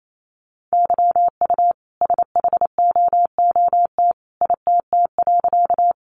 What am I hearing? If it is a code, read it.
Morse code, heard as a keyed tone, about 28 wpm, 700 Hz. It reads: YU SHOOT ITT.
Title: A Sound Of Thunder 28wpm